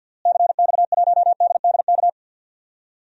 Morse code, heard as a keyed tone, about 52 wpm, 700 Hz. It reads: CX1DDK
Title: DXCC-100 Call Signs - 6 Characters - Repeated 3x - 50wpm